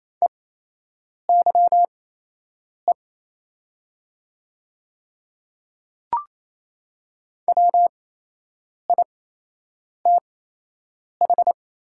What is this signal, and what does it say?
Morse code, heard as a keyed tone, about 28 wpm, spaced as for 8 wpm, 700 Hz. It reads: EYE WITH
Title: 8x Character Spacing - Top 300 Words - 28wpm